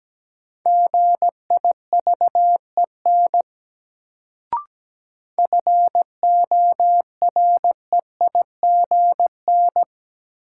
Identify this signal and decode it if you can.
Morse code, heard as a keyed tone, about 17 wpm, 700 Hz. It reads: GIVEN FOREIGN
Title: Sentences from Top 1400 Words - New Word Drill Alt - 17wpm